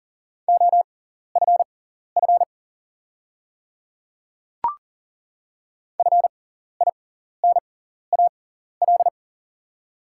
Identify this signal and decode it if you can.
Morse code, heard as a keyed tone, about 40 wpm, spaced as for 14 wpm, 690 Hz. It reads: OFF FINAL
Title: 6x Character Spacing - Top 500 Words - 40wpm